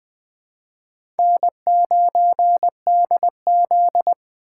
Morse code, heard as a keyed tone, about 20 wpm, 700 Hz. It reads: N9DZ
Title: USA Call Signs - 4 Characters - 2x Character Spacing - 20wpm